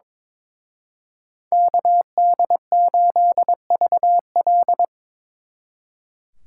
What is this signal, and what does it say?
Morse code, heard as a keyed tone, about 26 wpm, 700 Hz. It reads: KD8VL
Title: Speed Racing - USA 2x2 Call Signs 22wpm